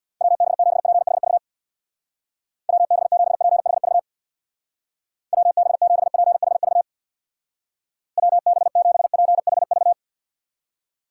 Morse code, heard as a keyed tone, about 61 wpm, 700 Hz. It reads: WB7PHV WB7PHV WB7PHV WB7PHV
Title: Speed Racing - USA 2x3 Call Signs 50wpm